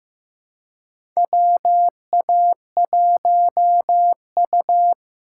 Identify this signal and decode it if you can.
Morse code, heard as a keyed tone, about 15 wpm, 700 Hz. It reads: WA1U
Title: USA Call Signs - 4 Characters - 4x Character Spacing - 15wpm